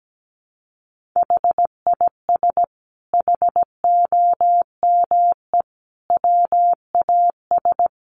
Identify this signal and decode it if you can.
Morse code, heard as a keyed tone, about 17 wpm, 700 Hz. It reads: HIS HOME WAS
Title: Sentences from Top 1500 Words 17wpm